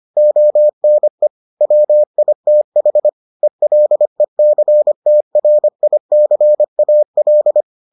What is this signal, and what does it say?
Morse code, heard as a keyed tone, about 25 wpm, 600 Hz. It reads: ONE WITH ELECTRICAL